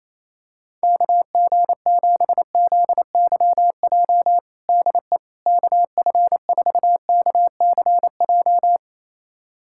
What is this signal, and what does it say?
Morse code, heard as a keyed tone, about 28 wpm, 700 Hz. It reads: KG7ZYJ DE KF4KCJ
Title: USA Callsign DE Callsign - 6 Characters - 28wpm